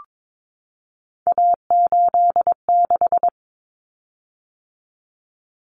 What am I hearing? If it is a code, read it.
Morse code, heard as a keyed tone, about 22 wpm, 700 Hz. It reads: A86
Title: Common Cut Numbers - 3-Digit 22wpm